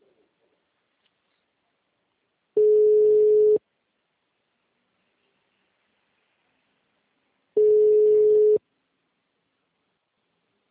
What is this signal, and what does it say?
Ringback tone (Europe, 425 Hz).